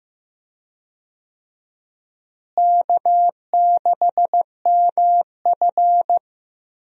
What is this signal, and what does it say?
Morse code, heard as a keyed tone, about 15 wpm, 700 Hz. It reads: K6MF
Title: USA Call Signs - 4 Characters - Repeated 3x - 15wpm